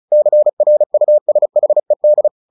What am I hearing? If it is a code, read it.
Morse code, heard as a keyed tone, about 34 wpm, 600 Hz. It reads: CRUSHED
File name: the old man and the sea 2 35wpm